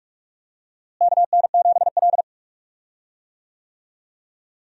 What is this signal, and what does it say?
Morse code, heard as a keyed tone, about 44 wpm, 700 Hz. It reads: KN7L
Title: USA Call Signs - 4 Characters - Repeated 3x - 45wpm